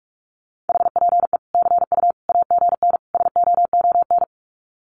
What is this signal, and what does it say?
Morse code, heard as a keyed tone, about 44 wpm, 700 Hz. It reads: HPE CU AGN SOON
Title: Common QSO Phrases 45wpm